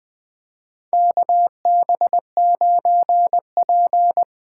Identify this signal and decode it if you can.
Morse code, heard as a keyed tone, about 20 wpm, 700 Hz. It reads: KB9P
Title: USA Call Signs - 6 Characters - 8x Character Spacing - 20wpm